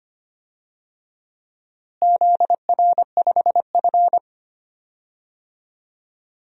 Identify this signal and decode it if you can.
Morse code, heard as a keyed tone, about 25 wpm, 700 Hz. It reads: ZR5F
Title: DXCC-100 Call Signs - 4 Characters - Repeated 3x - 25wpm